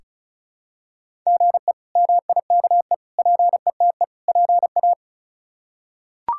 Morse code, heard as a keyed tone, about 36 wpm, 700 Hz. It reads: GE MIKE PETE PA
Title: K1USN SST QSO Practice - 35wpm